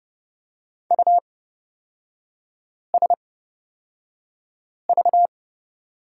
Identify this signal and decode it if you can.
Morse code, heard as a keyed tone, about 30 wpm, spaced as for 5 wpm, 700 Hz. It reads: USV
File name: Lesson 37e - Letter x - Three Characters 30wpm